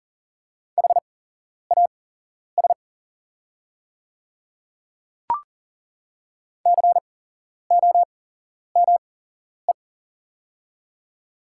Morse code, heard as a keyed tone, about 40 wpm, spaced as for 11 wpm, 690 Hz. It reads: HAS COME
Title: 8x Character Spacing - Top 100 Words - 40wpm